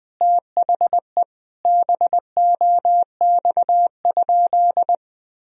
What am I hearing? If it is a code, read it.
Morse code, heard as a keyed tone, about 21 wpm, 700 Hz. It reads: THE BOX?
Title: Sentences from Top 1000 Words - Review Part 2 - 20wpm